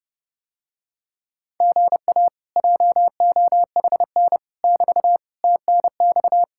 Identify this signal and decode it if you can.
Morse code, heard as a keyed tone, about 32 wpm, 700 Hz. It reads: GA JOHN <BT> TNX